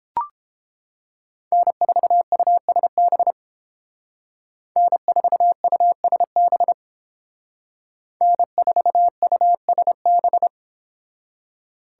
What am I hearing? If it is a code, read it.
Morse code, heard as a keyed tone, about 30 wpm, 700 Hz. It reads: N4USB N4USB N4USB